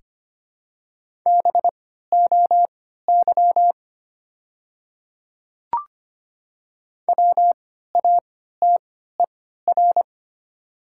Morse code, heard as a keyed tone, about 25 wpm, spaced as for 14 wpm, 700 Hz. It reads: BOY WATER
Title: 3x Character Spacing - Top 200 Words - 25wpm